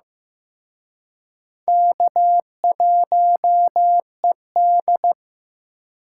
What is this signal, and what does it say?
Morse code, heard as a keyed tone, about 18 wpm, 700 Hz. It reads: K1ED
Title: Speed Racing - USA Call Signs 15wpm